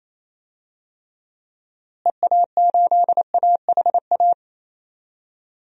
Morse code, heard as a keyed tone, about 28 wpm, 700 Hz. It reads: EA8AHA
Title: DXCC-100 Call Signs - 6 Characters - Repeated 2x - 28wpm